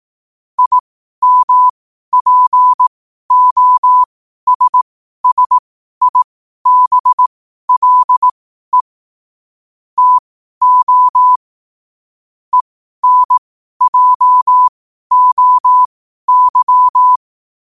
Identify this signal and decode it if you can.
Morse code, heard as a keyed tone, about 18 wpm, spaced as for 12 wpm, 1000 Hz. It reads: IMPOSSIBLE TO ENJOY
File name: qotdfiles / 12wpm